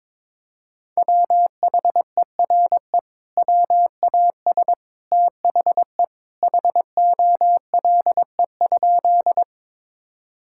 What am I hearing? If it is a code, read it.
Morse code, heard as a keyed tone, about 22 wpm, 700 Hz. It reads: WHERE WAS THE HOLE?
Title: Sentences from Top 1600 Words 22wpm